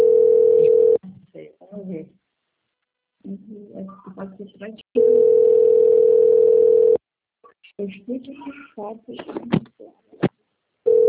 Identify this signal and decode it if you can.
Ringback tone (North America), 440 + 480 Hz, with speech.
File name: out- - - - - .19955